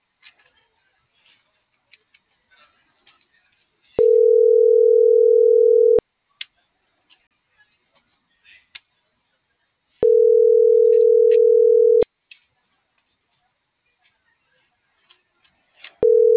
Ringback tone (North America), 440 + 480 Hz.